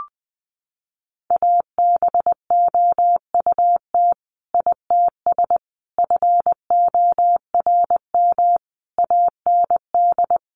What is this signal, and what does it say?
Morse code, heard as a keyed tone, about 20 wpm, 700 Hz. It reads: ABOUT ITS FORM AND
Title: Sentences from Top 1200 Words 20wpm